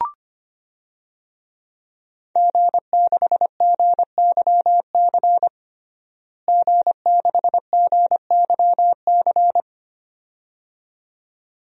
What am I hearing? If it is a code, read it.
Morse code, heard as a keyed tone, about 25 wpm, 700 Hz. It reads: G6GYC G6GYC
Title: DXCC-100 Call Signs - 5 Characters - Repeated 2x - 25wpm